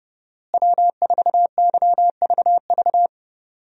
Morse code, heard as a keyed tone, about 30 wpm, 700 Hz. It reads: W4YVV
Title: USA Call Signs - 5 Characters - Repeated 2x - 30wpm